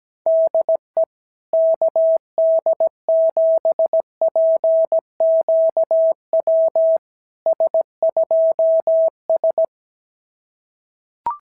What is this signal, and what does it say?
Morse code, heard as a keyed tone, about 17 wpm, 650 Hz. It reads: DE KD7PQW S2S
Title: SOTA QSO Practice 17wpm